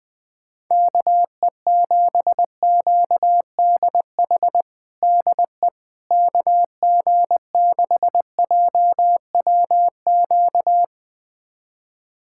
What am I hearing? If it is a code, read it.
Morse code, heard as a keyed tone, about 20 wpm, 700 Hz. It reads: KE7QDH DE KG6JWQ
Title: USA Callsign DE Callsign - 6 Characters - 20wpm